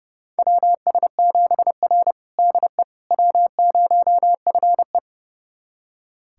Morse code, heard as a keyed tone, about 32 wpm, 700 Hz. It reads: WS7R DE W0FE
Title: USA Callsign DE Callsign - 4 Characters - 30wpm